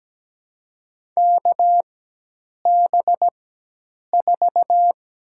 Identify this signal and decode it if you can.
Morse code, heard as a keyed tone, about 17 wpm, spaced as for 8 wpm, 700 Hz. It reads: KB4
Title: USA Call Signs - 6 Characters - 4x Character Spacing - 17wpm